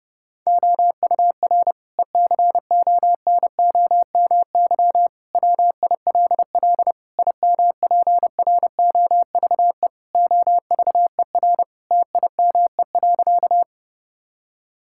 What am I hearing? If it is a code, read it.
Morse code, heard as a keyed tone, about 32 wpm, 700 Hz. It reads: OUR ECONOMY WILL IMPROVE OVER TIME.
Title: Sentences from Top 1000 Words - Review Part 5 - 30wpm